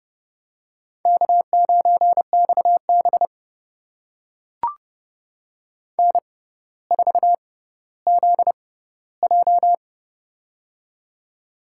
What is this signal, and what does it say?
Morse code, heard as a keyed tone, about 30 wpm, spaced as for 15 wpm, 690 Hz. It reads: K9XB N4ZJ
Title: USA Call Signs - 4 Characters - 6x Character Spacing - 30wpm